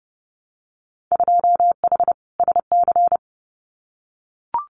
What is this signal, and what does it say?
Morse code, heard as a keyed tone, about 30 wpm, 700 Hz. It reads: 2H SC
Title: Winter Field Day Exchanges 30wpm